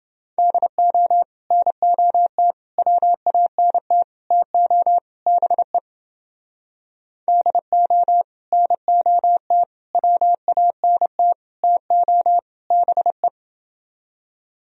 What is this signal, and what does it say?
Morse code, heard as a keyed tone, about 30 wpm, 700 Hz. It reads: DO NOT WANT TO BE DO NOT WANT TO BE